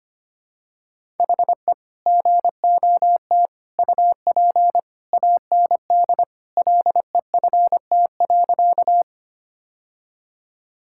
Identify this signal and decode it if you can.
Morse code, heard as a keyed tone, about 25 wpm, 700 Hz. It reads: HE GOT UP AND LEFT.
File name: practice / 25wpm / Sentences with 2-Word Phrases - 25wpm